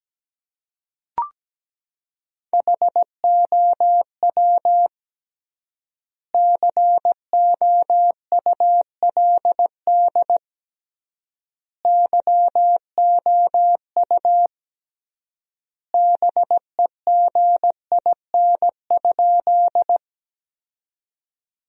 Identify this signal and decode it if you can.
Morse code, heard as a keyed tone, about 17 wpm, 700 Hz. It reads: HOW COULD YOU BEGIN?
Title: 3x Word-Spacing - Sentences from Top 300 Words - 17wpm